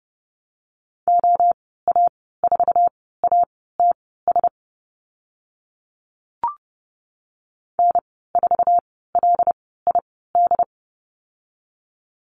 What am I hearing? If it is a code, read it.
Morse code, heard as a keyed tone, about 30 wpm, spaced as for 17 wpm, 700 Hz. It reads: OA4ATS N4LID
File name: Lesson 15g - Number 5 - Call Signs 30wpm